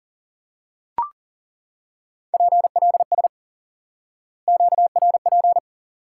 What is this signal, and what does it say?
Morse code, heard as a keyed tone, about 40 wpm, 700 Hz. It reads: PLS QRP
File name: IQE Course 3 - Lesson 89 - YAGI - Two Elements - 40wpm